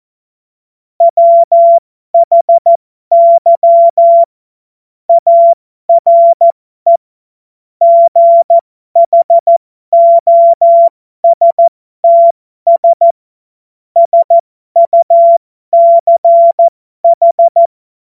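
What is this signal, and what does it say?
Morse code, heard as a keyed tone, about 14 wpm, spaced as for 12 wpm, 680 Hz. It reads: WHY ARE GHOSTS SUCH
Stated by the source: Morse joke practice at effective 60, Farnsworth 70.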